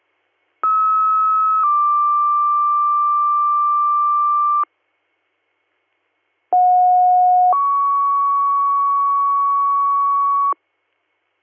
Two-tone page, heard as a one-second tone, then a three-second tone, about 1290 then 1180 Hz, about 730 then 1120 Hz. It is heard from 0.6 to 10.5 s.